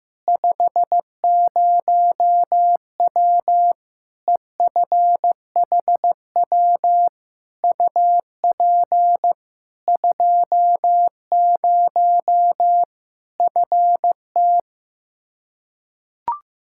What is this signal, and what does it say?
Morse code, heard as a keyed tone, about 15 wpm, 700 Hz. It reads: 50W EFHW UP 20 FT
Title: Practice QSO - Exchange 2 - Info - 15wpm